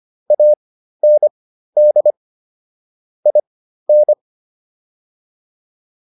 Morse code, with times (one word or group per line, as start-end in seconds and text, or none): 0.3-2.1 AND
3.3-4.1 IN